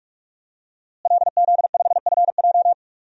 Morse code, heard as a keyed tone, about 41 wpm, 700 Hz. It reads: LZ5FJ
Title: DXCC-100 Call Signs - 5 Characters - Repeated 2x - 45wpm